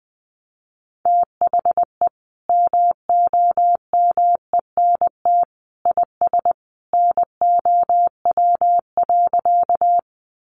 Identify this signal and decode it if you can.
Morse code, heard as a keyed tone, about 20 wpm, 700 Hz. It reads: THE MOMENT IS NOW.
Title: Sentences from Top 1000 Words 20wpm